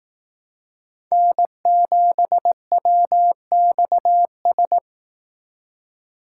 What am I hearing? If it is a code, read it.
Morse code, heard as a keyed tone, about 18 wpm, 700 Hz. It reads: N7WXS
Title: Speed Racing - USA Call Signs 15wpm